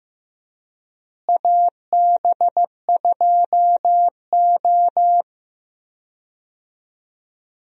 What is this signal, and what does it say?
Morse code, heard as a keyed tone, about 15 wpm, 700 Hz. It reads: AB2O
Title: USA Call Signs - 4 Characters - Repeated 3x - 15wpm